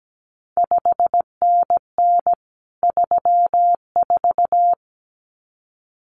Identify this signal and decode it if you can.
Morse code, heard as a keyed tone, about 17 wpm, 700 Hz. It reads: 5NN 34
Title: CQ WW DX Contest Exchange Encore - 17wpm